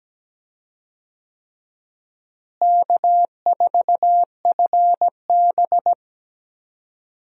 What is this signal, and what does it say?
Morse code, heard as a keyed tone, about 17 wpm, 700 Hz. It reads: K4FB
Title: USA Call Signs - 4 Characters - Repeated 3x - 17wpm